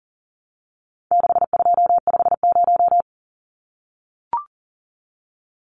Morse code, heard as a keyed tone, about 40 wpm, 700 Hz. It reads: 6250
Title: Numbers - 4-Digit 40wpm